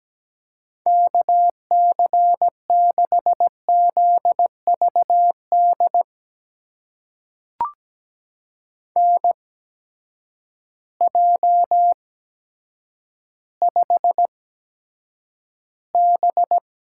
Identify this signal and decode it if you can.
Morse code, heard as a keyed tone, about 17 wpm, spaced as for 9 wpm, 700 Hz. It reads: KC6ZVD NJ5B
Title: USA Call Signs - 8x Character Spacing - 17wpm